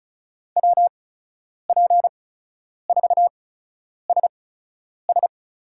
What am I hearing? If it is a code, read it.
Morse code, heard as a keyed tone, about 36 wpm, spaced as for 10 wpm, 700 Hz. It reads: WP4SS
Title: USA Call Signs - 5 Characters - 8x Character Spacing - 35wpm